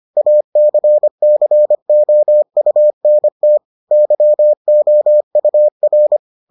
Morse code, heard as a keyed tone, about 25 wpm, 600 Hz. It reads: ACCOUNT YOUR